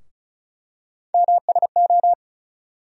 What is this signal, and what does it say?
Morse code, heard as a keyed tone, about 34 wpm, 700 Hz. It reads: MSO
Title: USA Call Sign Postfixes - 3 Letters 35wpm